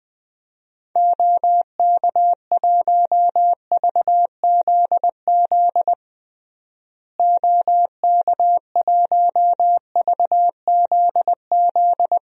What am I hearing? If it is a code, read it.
Morse code, heard as a keyed tone, about 20 wpm, 700 Hz. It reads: OK1VZZ OK1VZZ
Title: DXCC-100 Call Signs - 6 Characters - Repeated 2x - 20wpm